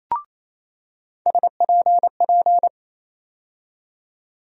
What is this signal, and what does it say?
Morse code, heard as a keyed tone, about 28 wpm, 700 Hz. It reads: SPP
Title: USA Call Sign Postfixes - 3 Letters 28wpm